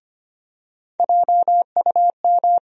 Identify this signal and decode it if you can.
Morse code, heard as a keyed tone, about 25 wpm, 700 Hz. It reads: JUM
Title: USA Call Sign Postfixes - 3 Letters 25wpm